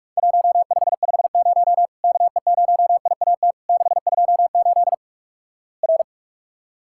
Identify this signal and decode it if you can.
Morse code, heard as a keyed tone, about 48 wpm, 700 Hz. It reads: 1550 KE0IAT 628 R
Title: Worked All Europe DX Contest - QTC Exchange - 45wpm